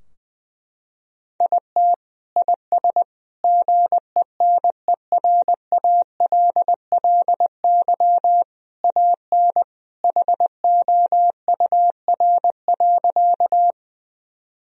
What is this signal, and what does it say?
Morse code, heard as a keyed tone, about 20 wpm, 700 Hz. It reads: IT IS GENERALLY AN HOUR.
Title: Sentences from Top 1000 Words - Review Part 4 - 20wpm